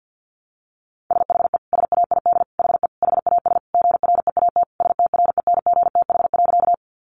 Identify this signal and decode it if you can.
Morse code, heard as a keyed tone, about 50 wpm, 700 Hz. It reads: SHE SAID HE HAS GREAT STRENGTH.
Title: Sentences from Top 1300 Words 50wpm